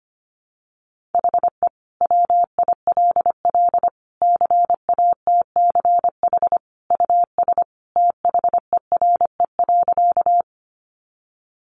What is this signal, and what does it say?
Morse code, heard as a keyed tone, about 25 wpm, 700 Hz. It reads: HE WILL CATCH US THERE.
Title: Sentences from Top 1500 Words 25wpm